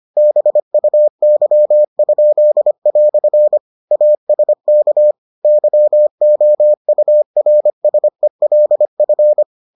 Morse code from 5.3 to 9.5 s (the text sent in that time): YOURSELF